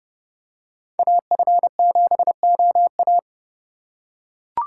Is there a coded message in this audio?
Yes: Morse code from 1.0 to 3.2 s, heard as a keyed tone, about 29 wpm, 700 Hz. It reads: AF7OA